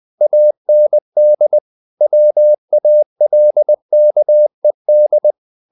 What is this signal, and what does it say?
Morse code, heard as a keyed tone, about 20 wpm, 600 Hz. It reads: AND WALKED
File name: the old man and the sea / the old man and the sea part 1 20wpm